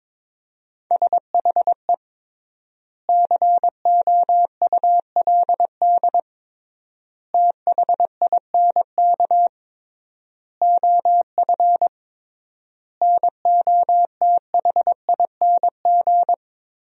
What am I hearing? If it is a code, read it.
Morse code, heard as a keyed tone, about 22 wpm, 700 Hz. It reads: SHE COULD THINK OF NOTHING
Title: 3x Word-Spacing - Sentences from Top 500 Words - 22wpm